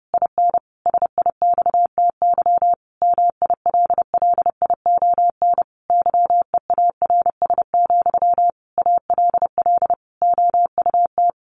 Morse code, heard as a keyed tone, about 30 wpm, 700 Hz. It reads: IN SIXTY MILLION YEARS, ALL OUT
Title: A Sound Of Thunder 30wpm